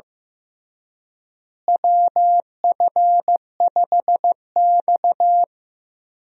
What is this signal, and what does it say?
Morse code, heard as a keyed tone, about 15 wpm, 700 Hz. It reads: WF5X
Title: USA Call Signs - 4 Characters - Repeated 3x - 15wpm